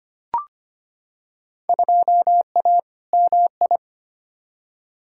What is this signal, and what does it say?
Morse code, heard as a keyed tone, about 25 wpm, 700 Hz. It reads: 2A MI
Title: Field Day Exchanges - 25wpm